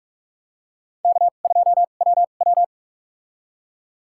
Morse code, heard as a keyed tone, about 45 wpm, spaced as for 32 wpm, 700 Hz. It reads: K2WW
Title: USA Call Signs - 2x Character Spacing - 45wpm